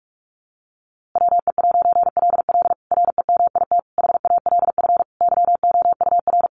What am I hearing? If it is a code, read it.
Morse code, heard as a keyed tone, about 44 wpm, 700 Hz. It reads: WE'LL REMIT HALF YOUR
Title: A Sound Of Thunder 45wpm